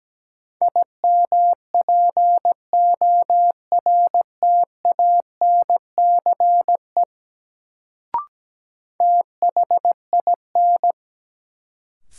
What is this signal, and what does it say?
Morse code, heard as a keyed tone, about 17 wpm, 700 Hz. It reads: IMPORTANCE THIN
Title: Sentences from Top 1200 Words - New Word Drill Alt - 17wpm